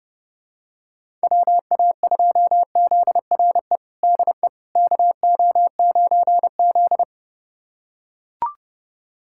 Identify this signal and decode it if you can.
Morse code, heard as a keyed tone, about 30 wpm, 700 Hz. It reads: WA2ZRE DE KO9Z
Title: USA Callsign DE Callsign 30wpm